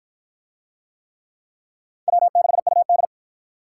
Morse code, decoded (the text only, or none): W6UD